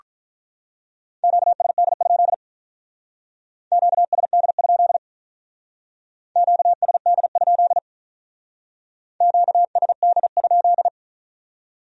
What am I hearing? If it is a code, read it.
Morse code, heard as a keyed tone, about 44 wpm, 700 Hz. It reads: QSD? QSD? QSD? QSD?